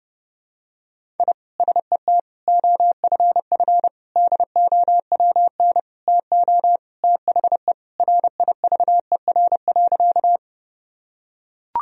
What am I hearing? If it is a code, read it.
Morse code, heard as a keyed tone, about 32 wpm, 700 Hz. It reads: I SET OFF DOWN TO THE RIVER.
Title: Sentences from Top 1000 Words - Review Part 1 - 30wpm